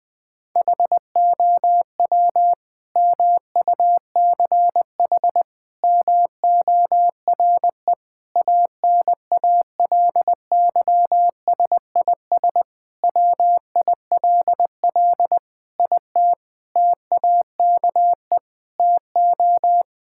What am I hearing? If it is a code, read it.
Morse code, heard as a keyed tone, about 20 wpm, 700 Hz. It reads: HOW MUCH MORE ANALYSIS WILL IT TAKE TO
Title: Sentences from Top 1000 Words - Review Part 5 - 20wpm